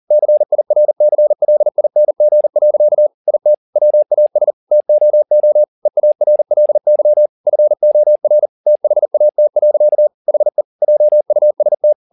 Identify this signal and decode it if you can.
Morse code, heard as a keyed tone, about 40 wpm, 600 Hz. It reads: CIRCLING. IT WAS TOO EARLY FOR THAT. HE JUST
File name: the old man and the sea 2 40wpm